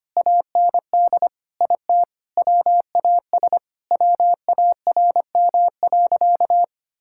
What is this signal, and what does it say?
Morse code, heard as a keyed tone, about 25 wpm, 700 Hz. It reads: AND IT WAS WARM.
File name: practice / 25wpm / Sentences with 2-Word Phrases - 25wpm